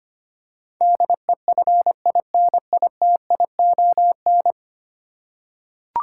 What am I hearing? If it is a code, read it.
Morse code, heard as a keyed tone, about 25 wpm, 700 Hz. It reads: DEFINITION
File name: 10 Letter Words - First 3 Letters Given - 25wpm